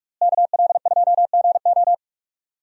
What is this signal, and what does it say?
Morse code, heard as a keyed tone, about 42 wpm, 700 Hz. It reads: KL2GO
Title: USA Call Signs - 5 Characters - Repeated 2x - 45wpm